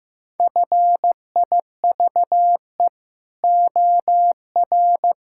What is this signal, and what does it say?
Morse code, heard as a keyed tone, about 15 wpm, 700 Hz. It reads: FIVE OR
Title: Pangram Challenge v2 - 15wpm